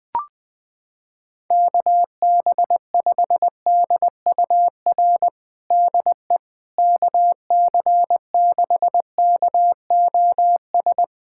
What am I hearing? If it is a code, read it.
Morse code, heard as a keyed tone, about 21 wpm, 700 Hz. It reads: KB5DUR DE KC6KOS
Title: USA Callsign DE Callsign - 6 Characters - 20wpm